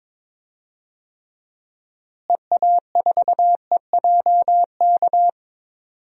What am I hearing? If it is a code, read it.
Morse code, heard as a keyed tone, about 22 wpm, 700 Hz. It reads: EA4EJK
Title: DXCC-100 Call Signs - 6 Characters - Repeated 2x - 22wpm